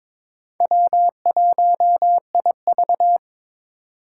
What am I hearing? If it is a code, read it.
Morse code, heard as a keyed tone, about 22 wpm, 700 Hz. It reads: W1IV
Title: Four Challenging Letter-Number Combinations 22wpm